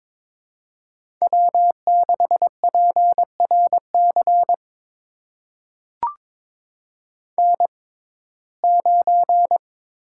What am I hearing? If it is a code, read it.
Morse code, heard as a keyed tone, about 23 wpm, spaced as for 16 wpm, 700 Hz. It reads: W6PRC N9
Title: USA Call Signs - 5 Characters - 6x Character Spacing - 22wpm